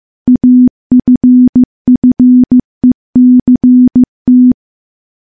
Morse code, heard as a keyed tone, about 15 wpm, 250 Hz. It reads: AFFECT
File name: morse / morse-training-15WPM